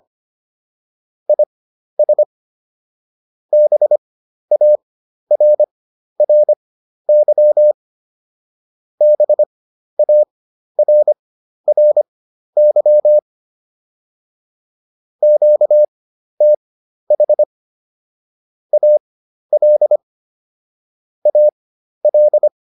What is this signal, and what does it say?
Morse code, heard as a keyed tone, about 25 wpm, spaced as for 12 wpm, 600 Hz. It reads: IS BARRY BARRY QTH AL AL